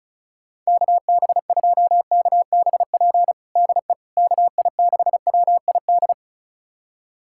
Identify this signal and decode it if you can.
Morse code, heard as a keyed tone, about 34 wpm, 700 Hz. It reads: KB2KBP DE KI6WID
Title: USA Callsign DE Callsign - 6 Characters - 35wpm